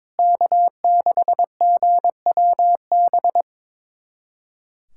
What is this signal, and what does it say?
Morse code, heard as a keyed tone, about 22 wpm, 700 Hz. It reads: K6GWB
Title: USA Callsign DE Callsign - 5 Characters - 22wpm